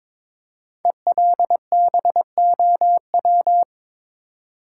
Morse code, heard as a keyed tone, about 22 wpm, 700 Hz. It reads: ELBOW
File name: 5 Letter Words - First 3 Letters Given - 22wpm